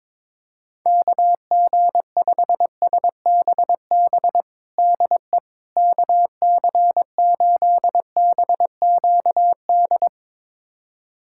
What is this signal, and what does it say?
Morse code, heard as a keyed tone, about 22 wpm, 700 Hz. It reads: KG5SBB DE KC8BQD
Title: USA Callsign DE Callsign - 6 Characters - 22wpm